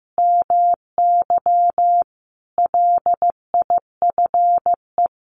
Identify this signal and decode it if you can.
Morse code, heard as a keyed tone, about 14 wpm, 700 Hz. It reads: MY LIFE
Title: Sets of 5 Words Encore2 15wpm